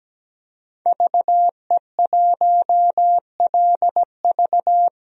Morse code, heard as a keyed tone, about 17 wpm, 700 Hz. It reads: VE1LV
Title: DXCC-100 Call Signs - 5 Characters - 17wpm